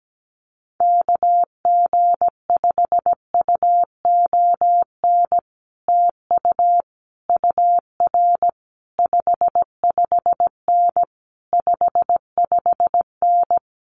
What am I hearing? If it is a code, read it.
Morse code, heard as a keyed tone, about 17 wpm, 700 Hz. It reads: KG5UON TU UR 55N 55N
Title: Common QSO Phrases Encore 17wpm